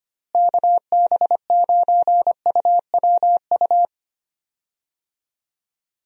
Morse code, heard as a keyed tone, about 25 wpm, 700 Hz. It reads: KB9UWU
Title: USA Call Signs - 6 Characters - Repeated 2x - 25wpm